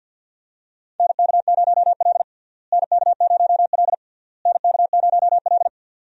Morse code, heard as a keyed tone, about 52 wpm, 700 Hz. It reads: NK0L NK0L NK0L
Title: USA Call Signs - 4 Characters - Repeated 3x - 50wpm